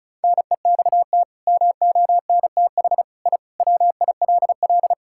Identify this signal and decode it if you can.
Morse code, heard as a keyed tone, about 36 wpm, 700 Hz. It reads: NEXT MONTH I WILL
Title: Sentences from Top 1000 Words - Review Part 5 - 35wpm